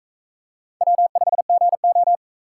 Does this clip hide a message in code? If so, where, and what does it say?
Yes: Morse code from 0.8 to 2.2 s, heard as a keyed tone, about 39 wpm, 700 Hz. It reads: W5GO